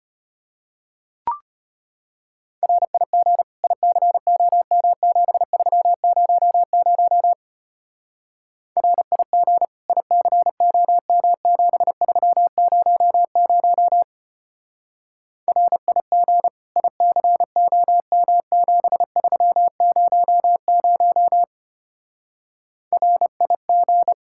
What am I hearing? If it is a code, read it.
Morse code, heard as a keyed tone, about 32 wpm, 700 Hz. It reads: RIG ICOM7300 RIG ICOM7300 RIG ICOM7300 RIG